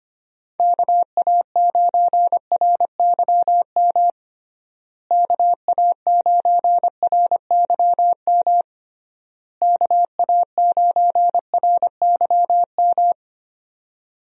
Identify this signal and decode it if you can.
Morse code, heard as a keyed tone, about 25 wpm, 700 Hz. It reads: KA9RYM KA9RYM KA9RYM
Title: USA Call Signs - 6 Characters - Repeated 3x - 25wpm